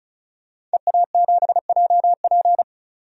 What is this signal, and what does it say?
Morse code, heard as a keyed tone, about 34 wpm, 700 Hz. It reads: EA7JP
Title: DXCC-100 Call Signs - 5 Characters - 35wpm